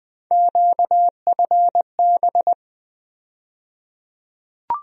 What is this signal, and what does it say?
Morse code, heard as a keyed tone, about 21 wpm, 700 Hz. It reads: QFB
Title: USA Call Sign Postfixes - 3 Letters 20wpm